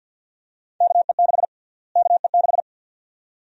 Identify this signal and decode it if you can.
Morse code, heard as a keyed tone, about 46 wpm, 700 Hz. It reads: KE6 KE6
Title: USA Call Sign Prefixes - Repeated 2x - 50wpm